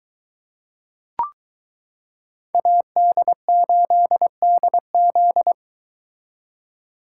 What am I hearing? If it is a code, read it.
Morse code, heard as a keyed tone, about 22 wpm, 700 Hz. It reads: AD8DZ